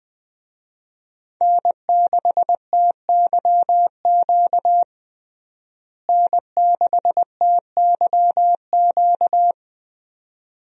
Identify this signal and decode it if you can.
Morse code, heard as a keyed tone, about 20 wpm, 700 Hz. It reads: N6TYQ N6TYQ
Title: USA Call Signs - 5 Characters - Repeated 2x - 20wpm